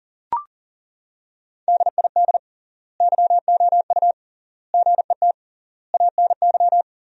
Morse code, heard as a keyed tone, about 37 wpm, 700 Hz. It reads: DID YOU GET ANY
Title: 3x Word-Spacing - Sentences from Top 400 Words - 40wpm